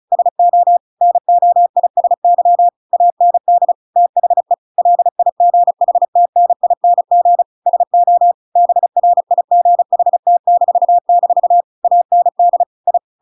Morse code, heard as a keyed tone, about 34 wpm, 700 Hz. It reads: SO NOISY AND THE LIGHTNING SO BRIGHT-- AND I
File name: Pinocchio / Pinocchio ch7 35 35 ik1hge